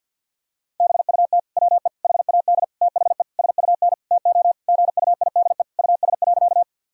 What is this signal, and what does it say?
Morse code, heard as a keyed tone, about 50 wpm, 700 Hz. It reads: BUT WE HAD THE SUN TO GUIDE US.
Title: Sentences from Top 1000 Words - Review Part 3 - 50wpm